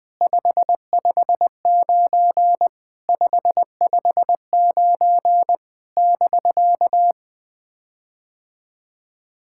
Morse code, heard as a keyed tone, about 20 wpm, 700 Hz. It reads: 559 559 <BK>
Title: POTA QSO Practice - 20wpm